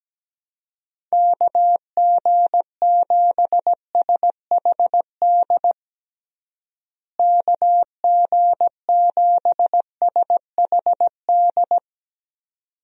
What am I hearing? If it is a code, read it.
Morse code, heard as a keyed tone, about 17 wpm, 700 Hz. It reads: KG7SHD KG7SHD
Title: USA Call Signs - 6 Characters - Repeated 2x - 17wpm